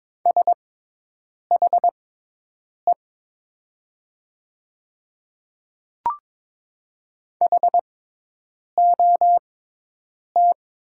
Morse code, heard as a keyed tone, about 22 wpm, spaced as for 8 wpm, 700 Hz. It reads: SHE HOT
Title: 6x Character Spacing - Top 200 Words - 22wpm